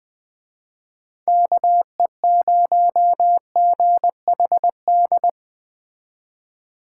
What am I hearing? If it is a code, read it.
Morse code, heard as a keyed tone, about 21 wpm, 700 Hz. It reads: KE0GHD
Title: Speed Racing - USA 2x3 Call Signs 17wpm